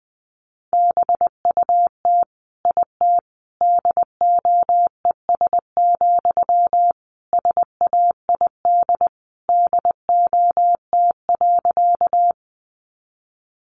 Morse code, heard as a keyed tone, about 20 wpm, 700 Hz. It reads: BUT IT DOES, SAID DOT.
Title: Sentences from Top 1500 Words 20wpm